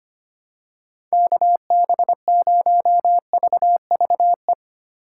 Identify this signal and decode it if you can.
Morse code, heard as a keyed tone, about 25 wpm, 700 Hz. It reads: KB0VVE
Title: USA Call Signs - 6 Characters - 3x Character Spacing - 25wpm